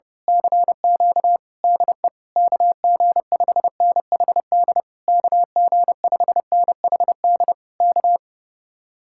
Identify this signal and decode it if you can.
Morse code, heard as a keyed tone, about 29 wpm, 700 Hz. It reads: CQ DE KG5NHD KG5NHD K
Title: Speed-Racing Common QSO Phrases - 30wpm